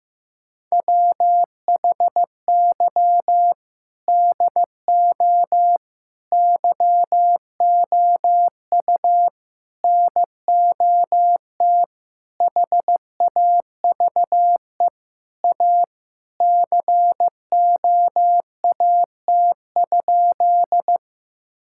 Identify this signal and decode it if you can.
Morse code, heard as a keyed tone, about 15 wpm, 700 Hz. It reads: WHY DO YOU NOT HAVE A COAT?
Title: Sentences from Top 1600 Words 15wpm